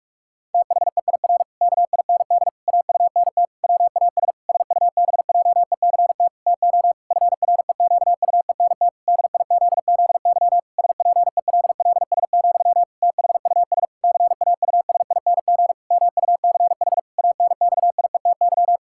Morse code, heard as a keyed tone, about 41 wpm, 700 Hz. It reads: THEIR KIND AUNT WAS SUBJECT TO FREQUENT DIZZY SPELLS, THUS CAUSING MUCH ANXIETY
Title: Pangram Challenge v2 - 45wpm